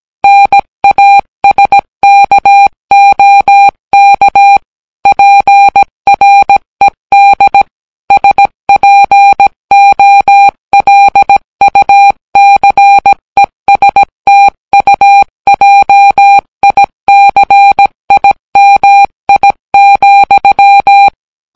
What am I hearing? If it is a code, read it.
Morse code, heard as a keyed tone, about 17 wpm, 800 Hz. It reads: NASKOK PRED SPOLUCESTUJICIMI,